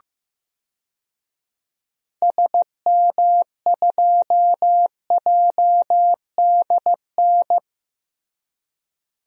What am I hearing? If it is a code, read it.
Morse code, heard as a keyed tone, about 16 wpm, 700 Hz. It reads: SM2JDN
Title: DXCC-100 Call Signs - 6 Characters - Repeated 2x - 15wpm